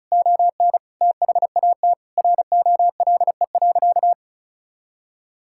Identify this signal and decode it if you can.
Morse code, heard as a keyed tone, about 36 wpm, 700 Hz. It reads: ON THAT ROLE.
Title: Sentences from Top 1000 Words - Review Part 6 - 35wpm